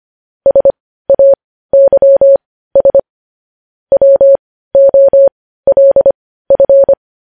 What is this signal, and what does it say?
Morse code, heard as a keyed tone, about 25 wpm, spaced as for 15 wpm, 550 Hz. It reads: SAYS WOLF
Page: News Headlines in Morse Code at 15 WPM